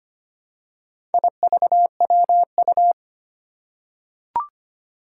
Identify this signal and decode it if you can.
Morse code, heard as a keyed tone, about 25 wpm, 700 Hz. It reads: IVWU
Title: Four Challenging Letter-Number Combinations 25wpm